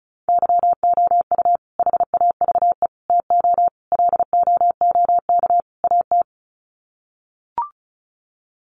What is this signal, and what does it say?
Morse code, heard as a keyed tone, about 34 wpm, 700 Hz. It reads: YOU HAVE TO LOOK AT
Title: Sets of 5 Words Encore 35wpm